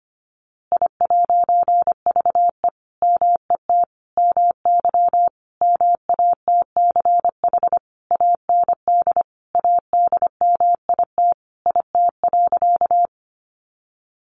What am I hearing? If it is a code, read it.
Morse code, heard as a keyed tone, about 25 wpm, 700 Hz. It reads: I'VE MET MY MATCH AND ADMIT IT.
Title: A Sound Of Thunder 25wpm